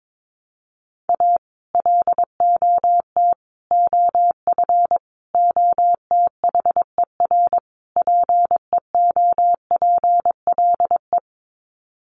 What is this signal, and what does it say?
Morse code, heard as a keyed tone, about 22 wpm, 700 Hz. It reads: A LOT OF OTHER PEOPLE
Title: Sets of 5 Words Encore2 22wpm